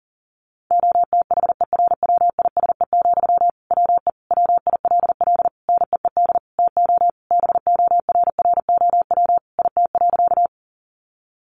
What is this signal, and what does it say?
Morse code, heard as a keyed tone, about 40 wpm, 700 Hz. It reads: OTHERWISE, WE WILL NEED TO BORROW IT.
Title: Sentences from Top 1500 Words 40wpm